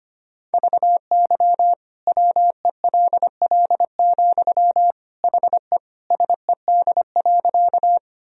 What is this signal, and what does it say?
Morse code, heard as a keyed tone, about 25 wpm, 700 Hz. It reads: VY WELL, HE SED.
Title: Sentences from Top 200 Words and Abbreviations 25wpm